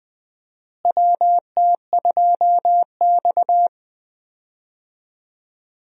Morse code, heard as a keyed tone, about 20 wpm, 700 Hz. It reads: WT2X
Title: USA Call Signs - 4 Characters - Repeated 2x - 20wpm